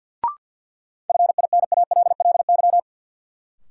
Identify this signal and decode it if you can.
Morse code, heard as a keyed tone, about 47 wpm, 700 Hz. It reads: FINALLY
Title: Sentences from Top 1000 Words - New Word Drill - 50wpm